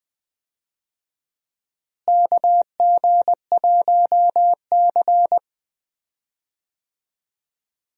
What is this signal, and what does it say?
Morse code, heard as a keyed tone, about 20 wpm, 700 Hz. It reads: KG1C
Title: USA Call Signs - 4 Characters - Repeated 3x - 20wpm